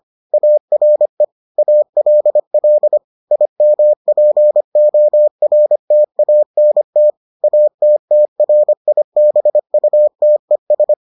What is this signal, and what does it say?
Morse code, heard as a keyed tone, about 25 wpm, 600 Hz. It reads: ARE ALL IMPORTANT ATTRIBUTES